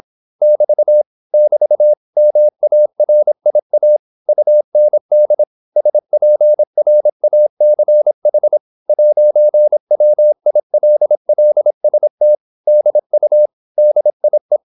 Morse code, heard as a keyed tone, about 26 wpm, 600 Hz. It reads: <BT> <BT> MARIA UND SPRACH 'WILLST DU DIE